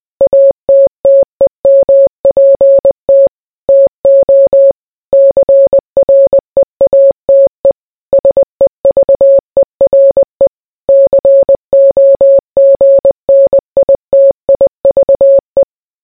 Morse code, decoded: ATTEMPT TO CREATE SEVERE COGNITIVE